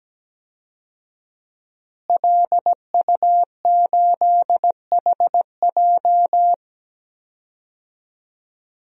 Morse code, heard as a keyed tone, about 17 wpm, 700 Hz. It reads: LU8HJ
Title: DXCC-100 Call Signs - 5 Characters - Repeated 3x - 17wpm